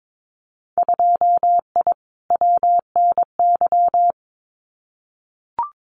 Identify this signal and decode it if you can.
Morse code, heard as a keyed tone, about 22 wpm, 700 Hz. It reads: 2I WNY
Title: Winter Field Day Exchanges 22wpm